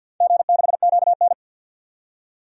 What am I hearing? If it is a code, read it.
Morse code, heard as a keyed tone, about 46 wpm, 700 Hz. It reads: G6QN
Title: DXCC-100 Call Signs - 4 Characters - 50wpm